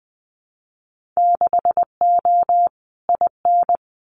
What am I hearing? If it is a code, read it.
Morse code, heard as a keyed tone, about 20 wpm, 700 Hz. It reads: 6O IN
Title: Winter Field Day Exchanges Encore 20wpm